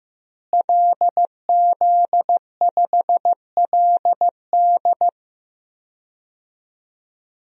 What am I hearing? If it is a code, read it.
Morse code, heard as a keyed tone, about 16 wpm, 700 Hz. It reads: LZ5LD
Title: DXCC-100 Call Signs - 5 Characters - Repeated 2x - 15wpm